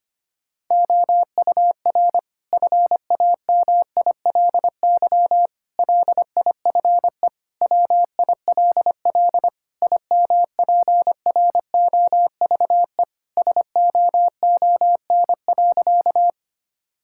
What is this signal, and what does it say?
Morse code, heard as a keyed tone, about 25 wpm, 700 Hz. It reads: OUR FAMILY LIFE WILL IMPROVE SOON.
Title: Sentences from Top 1000 Words - Review Part 5 - 25wpm